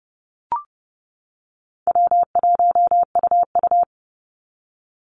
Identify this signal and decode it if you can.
Morse code, heard as a keyed tone, about 30 wpm, 700 Hz. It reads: W1UU
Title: Easy Call Signs - 30wpm